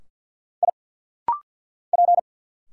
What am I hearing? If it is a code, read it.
Morse code, heard as a keyed tone, about 50 wpm, 700 Hz. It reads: I P
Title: Single Letters - Rapid-Fire 50wpm